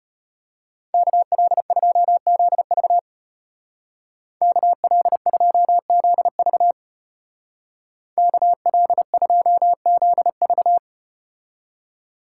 Morse code, heard as a keyed tone, about 32 wpm, 700 Hz. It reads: KL2ZV KL2ZV KL2ZV